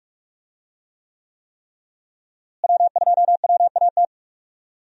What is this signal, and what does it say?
Morse code, heard as a keyed tone, about 45 wpm, 700 Hz. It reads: W2WAT
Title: USA Call Signs - 5 Characters - Repeated 3x - 45wpm